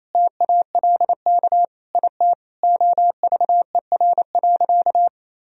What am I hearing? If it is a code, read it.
Morse code, heard as a keyed tone, about 28 wpm, 700 Hz. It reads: TALK IT OVER.
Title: Sentences from Top 500 Words and Abbreviations 28wpm